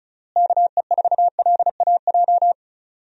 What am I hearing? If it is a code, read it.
Morse code, heard as a keyed tone, about 36 wpm, 700 Hz. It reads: KE4LAJ
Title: USA Call Signs - 6 Characters - Repeated 3x - 35wpm